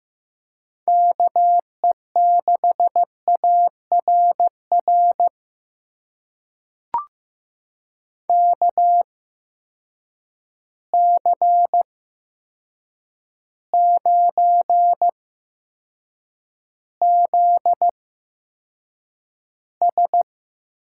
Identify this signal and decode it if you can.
Morse code, heard as a keyed tone, about 15 wpm, spaced as for 7 wpm, 700 Hz. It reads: KE6ARR KC9ZS
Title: USA Call Signs - 6 Characters - 8x Character Spacing - 15wpm